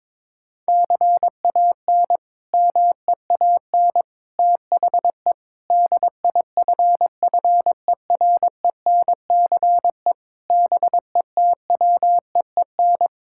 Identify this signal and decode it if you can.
Morse code, heard as a keyed tone, about 22 wpm, 700 Hz. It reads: CAN MEAN THE DIFFERENCE BETWEEN
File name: practice / 22wpm / Sets of 8 Words - 22wpm